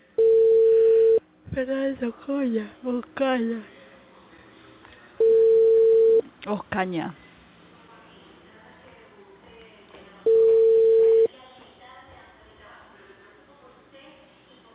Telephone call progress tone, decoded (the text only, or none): ringback tone